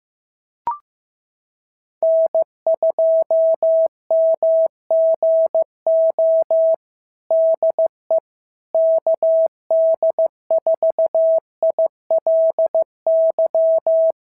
Morse code, heard as a keyed tone, about 15 wpm, 650 Hz. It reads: N2MGO DE KD4ILY
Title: SOTA QSO Practice 15wpm